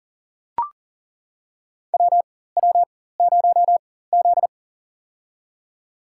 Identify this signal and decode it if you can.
Morse code, heard as a keyed tone, about 39 wpm, spaced as for 19 wpm, 700 Hz. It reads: WW0Z
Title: USA Call Signs - 4 Characters - 4x Character Spacing - 40wpm